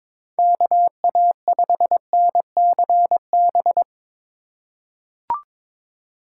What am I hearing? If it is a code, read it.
Morse code, heard as a keyed tone, about 22 wpm, 700 Hz. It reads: KA5NCB
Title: USA Call Signs - 6 Characters - 3x Character Spacing - 22wpm